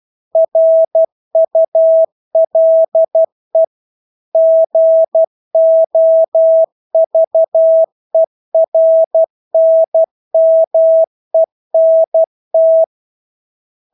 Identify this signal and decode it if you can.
Morse code, heard as a keyed tone, about 12 wpm, 650 Hz. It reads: RULE GOVERNMENT
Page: Engelske ord 12 wpm